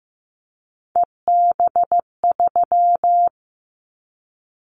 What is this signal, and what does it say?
Morse code, heard as a keyed tone, about 15 wpm, 700 Hz. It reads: EB3
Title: Callsign Prefixes for Top 25 DXCC - 15wpm